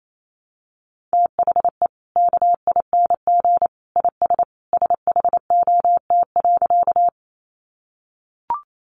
Morse code, heard as a keyed tone, about 28 wpm, 700 Hz. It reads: THE KING IS SHOT.
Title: Sentences from Top 1100 Words 28wpm